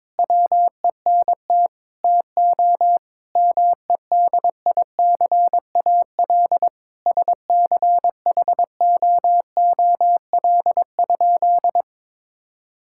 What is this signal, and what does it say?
Morse code, heard as a keyed tone, about 22 wpm, 700 Hz. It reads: WENT TO MEDICAL SCHOOL?
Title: Sentences from Top 1000 Words - Review Part 5 - 22wpm